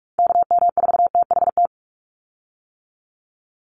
Morse code, heard as a keyed tone, about 44 wpm, 700 Hz. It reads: KM4THT
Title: USA Call Signs 45wpm